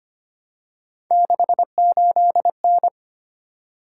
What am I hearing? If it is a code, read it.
Morse code, heard as a keyed tone, about 25 wpm, 700 Hz. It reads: 68N
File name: Three Letter-Number Combinations - 25wpm